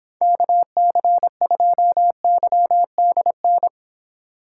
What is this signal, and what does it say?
Morse code, heard as a keyed tone, about 26 wpm, 700 Hz. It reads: KC2YDN